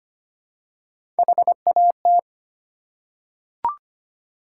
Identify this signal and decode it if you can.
Morse code, heard as a keyed tone, about 25 wpm, 700 Hz. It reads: HAT
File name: practice / 25wpm / Easy Words with Hints - 25wpm